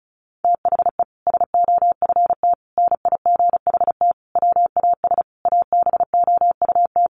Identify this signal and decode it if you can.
Morse code, heard as a keyed tone, about 34 wpm, 700 Hz. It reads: THE SOFT NIGHT WAS ABOUT
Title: Sentences from Top 1500 Words 35wpm